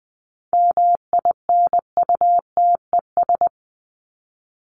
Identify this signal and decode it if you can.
Morse code, heard as a keyed tone, about 20 wpm, 700 Hz. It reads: MINUTES
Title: ES:S Suffix - Top 100 Words with Base Words 20wpm